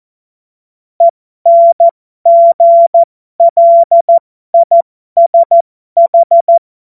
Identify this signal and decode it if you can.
Morse code, heard as a keyed tone, about 14 wpm, spaced as for 12 wpm, 680 Hz. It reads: ENGLISH
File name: Morsekurs / Witze